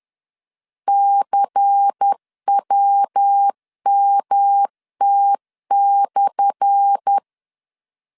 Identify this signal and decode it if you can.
Morse code, heard as a keyed tone, about 10 wpm, 800 Hz. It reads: CWMT/
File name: LEZIONI / 10 SIGNS / 10A SLASH 10wpm 2808